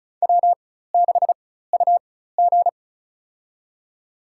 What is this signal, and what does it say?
Morse code, heard as a keyed tone, about 34 wpm, spaced as for 16 wpm, 700 Hz. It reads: W6UG
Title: USA Call Signs - 4x Character Spacing - 35wpm